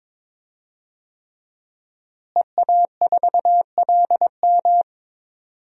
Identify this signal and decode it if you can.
Morse code, heard as a keyed tone, about 22 wpm, 700 Hz. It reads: EA4LM
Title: DXCC-100 Call Signs - 5 Characters - Repeated 3x - 22wpm